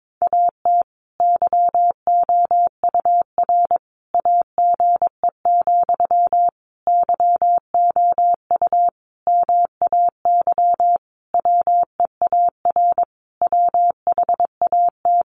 Morse code, heard as a keyed tone, about 22 wpm, 700 Hz. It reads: AT YOUR AGE, YOU MAY WEAR WHAT
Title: Sentences from Top 1500 Words 22wpm